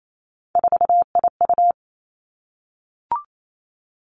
Morse code, heard as a keyed tone, about 28 wpm, 700 Hz. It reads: 4IU
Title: Three Challenging Letter-Number Combinations 28wpm